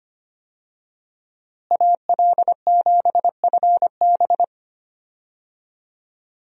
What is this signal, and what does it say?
Morse code, heard as a keyed tone, about 25 wpm, 700 Hz. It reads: AL7FB
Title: USA Call Signs - 5 Characters - Repeated 2x - 25wpm